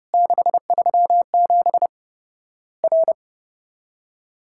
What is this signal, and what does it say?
Morse code, heard as a keyed tone, about 30 wpm, 700 Hz. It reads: 637 R
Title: Worked All Europe DX Contest - QTC Exchange - 30wpm